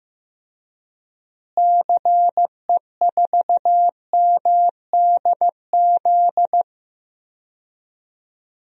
Morse code, heard as a keyed tone, about 15 wpm, 700 Hz. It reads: CE4MDZ
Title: DXCC-100 Call Signs - 6 Characters - Repeated 3x - 15wpm